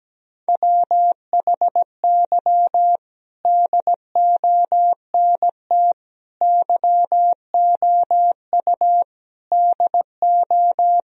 Morse code, heard as a keyed tone, about 17 wpm, 700 Hz. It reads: WHY DONT YOU DO
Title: Sentences from Top 1000 Words - Review Part 4 - 17wpm